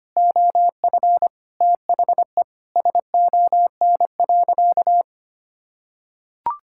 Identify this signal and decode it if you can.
Morse code, heard as a keyed tone, about 25 wpm, 700 Hz. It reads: OF THE SON.
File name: practice / 25wpm / Sentences with 9-Word Phrases - 25wpm